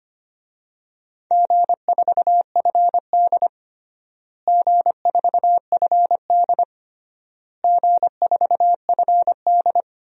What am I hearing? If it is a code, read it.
Morse code, heard as a keyed tone, about 25 wpm, 700 Hz. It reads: G4FD G4FD G4FD
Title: DXCC-100 Call Signs - 4 Characters - Repeated 3x - 25wpm